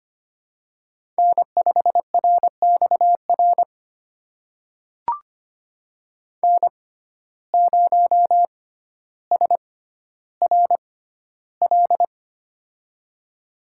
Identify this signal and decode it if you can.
Morse code, heard as a keyed tone, about 25 wpm, spaced as for 13 wpm, 700 Hz. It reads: N5RXR N0SRL
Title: USA Call Signs - 5 Characters - 6x Character Spacing - 25wpm